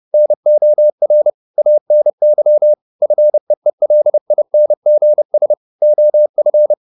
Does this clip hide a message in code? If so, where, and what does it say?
Yes: Morse code from 0.1 to 6.7 s, heard as a keyed tone, about 30 wpm, 600 Hz. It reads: NOR ANY FEELINGS OF